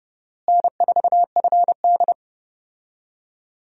Morse code, heard as a keyed tone, about 32 wpm, 700 Hz. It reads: N4FD
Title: USA Call Signs - 4 Characters - Repeated 2x - 30wpm